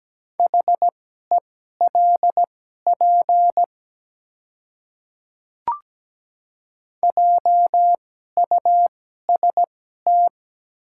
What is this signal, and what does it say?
Morse code, heard as a keyed tone, about 17 wpm, spaced as for 12 wpm, 700 Hz. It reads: HELP JUST
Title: 2x Character Spacing - Top 200 Words - 17wpm